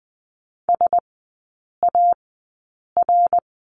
Morse code, heard as a keyed tone, about 20 wpm, spaced as for 8 wpm, 690 Hz. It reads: SAR
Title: 081 Lesson 14e - Number 2 - Three Characters 20wpm